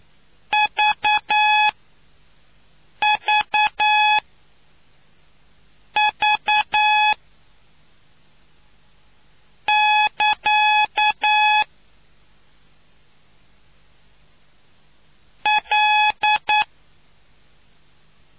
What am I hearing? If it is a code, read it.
Morse code, heard as a keyed tone, about 9 wpm, spaced as for 4 wpm, 850 Hz. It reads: VVV <KA> L